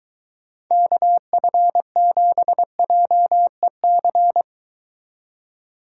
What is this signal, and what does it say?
Morse code, heard as a keyed tone, about 23 wpm, 700 Hz. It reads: KF7JEC